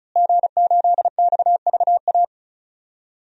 Morse code, heard as a keyed tone, about 36 wpm, 700 Hz. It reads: G8XVA
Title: DXCC-100 Call Signs - 5 Characters - Repeated 2x - 35wpm